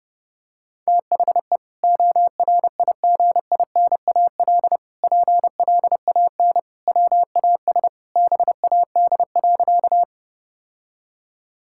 Morse code, heard as a keyed tone, about 30 wpm, 700 Hz. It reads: THE ORIGINAL PLAN WAS BAD.
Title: Sentences from Top 1000 Words - Review Part 6 - 30wpm